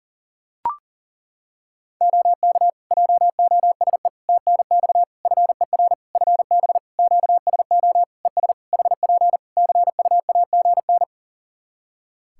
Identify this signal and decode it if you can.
Morse code, heard as a keyed tone, about 37 wpm, 700 Hz. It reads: OK JOSE TNX FER FB QSO ES HP CUAGN
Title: K1USN SST QSO Practice 40wpm